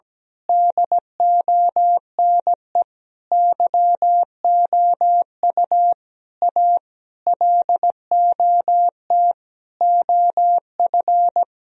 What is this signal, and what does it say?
Morse code, heard as a keyed tone, about 17 wpm, 700 Hz. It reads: DONE YOU A LOT OF
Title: Sentences from Top 1000 Words - Review Part 3 - 17wpm